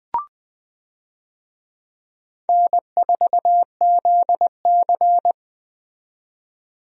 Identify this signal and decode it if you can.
Morse code, heard as a keyed tone, about 20 wpm, 700 Hz. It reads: N4ZC
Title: USA Call Signs - 4 Characters - 20wpm